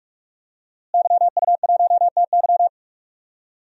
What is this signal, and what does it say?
Morse code, heard as a keyed tone, about 41 wpm, 700 Hz. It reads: YU1TY
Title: DXCC-100 Call Signs - 5 Characters - Repeated 2x - 45wpm